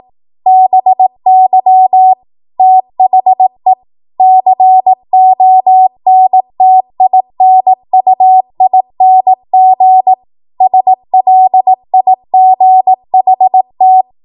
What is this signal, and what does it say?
Morse code, heard as a keyed tone, about 18 wpm, 750 Hz. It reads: BY THE CONTINUING SLIGHT
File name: Morse / 18WPM